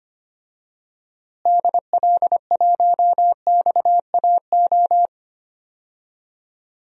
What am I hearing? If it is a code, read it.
Morse code, heard as a keyed tone, about 25 wpm, 700 Hz. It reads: DL1XAO
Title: DXCC-100 Call Signs - 6 Characters - Repeated 2x - 25wpm